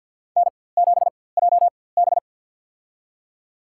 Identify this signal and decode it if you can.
Morse code, heard as a keyed tone, about 52 wpm, spaced as for 23 wpm, 700 Hz. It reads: N7JB
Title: USA Call Signs - 4x Character Spacing - 50wpm